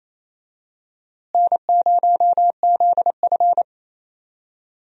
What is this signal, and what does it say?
Morse code, heard as a keyed tone, about 28 wpm, 700 Hz. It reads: N0ZF
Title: USA Call Signs - 4 Characters - Repeated 3x - 28wpm